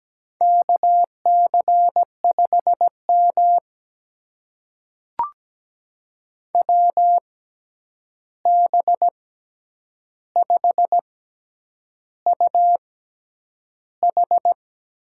Morse code, heard as a keyed tone, about 17 wpm, spaced as for 8 wpm, 700 Hz. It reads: KC5M WB5UH
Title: USA Call Signs - 6x Character Spacing - 17wpm